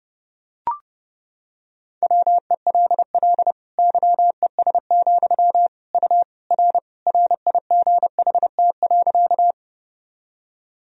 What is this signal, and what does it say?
Morse code, heard as a keyed tone, about 29 wpm, 700 Hz. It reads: WELL YES, U R RIGHT.
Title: Sentences from Top 500 Words and Abbreviations 30wpm